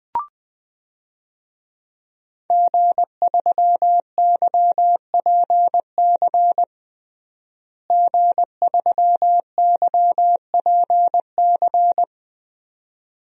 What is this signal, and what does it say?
Morse code, heard as a keyed tone, about 21 wpm, 700 Hz. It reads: G3YPC G3YPC
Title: DXCC-100 Call Signs - 5 Characters - Repeated 2x - 20wpm